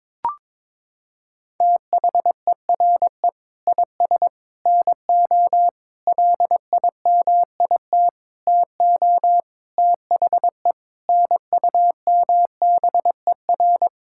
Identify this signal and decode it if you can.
Morse code, heard as a keyed tone, about 22 wpm, 700 Hz. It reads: THERE IS NO LIMIT TO THE NUMBER
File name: practice / 22wpm / Sets of 8 Words - 22wpm